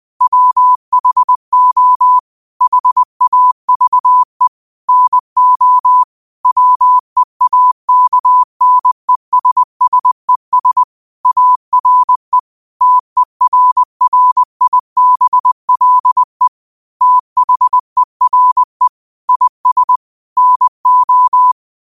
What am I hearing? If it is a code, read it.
Morse code, heard as a keyed tone, about 20 wpm, 1000 Hz. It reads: WHO HAVE NO WEAKNESSES ARE TERRIBLE THERE IS NO
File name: qotdfiles / 20wpm